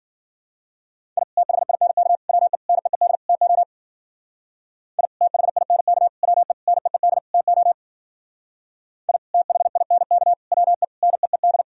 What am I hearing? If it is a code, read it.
Morse code, heard as a keyed tone, about 52 wpm, 700 Hz. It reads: I THINK WE NEED TO I THINK WE NEED TO I THINK WE NEED